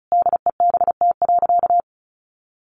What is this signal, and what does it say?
Morse code, heard as a keyed tone, about 34 wpm, 700 Hz. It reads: DEBT.
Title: Sentences from Top 1300 Words 35wpm